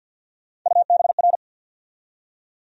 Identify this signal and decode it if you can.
Morse code, heard as a keyed tone, about 53 wpm, 700 Hz. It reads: UBR
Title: USA Call Sign Postfixes - 3 Letters 50wpm